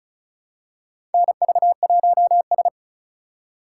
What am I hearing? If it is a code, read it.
Morse code, heard as a keyed tone, about 36 wpm, 700 Hz. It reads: NV1S
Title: USA Call Signs - 4 Characters - 2x Character Spacing - 35wpm